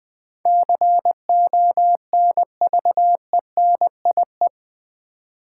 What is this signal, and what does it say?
Morse code, heard as a keyed tone, about 20 wpm, 700 Hz. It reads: CONVENIE
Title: Build A Word - 4 Letters At A Time - 11-Letter Words - 20wpm